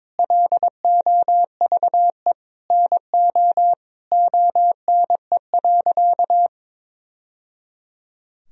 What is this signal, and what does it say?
Morse code, heard as a keyed tone, about 22 wpm, 700 Hz. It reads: LOVE NO ONE.
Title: Sentences from Top 1000 Words - Review Part 4 - 22wpm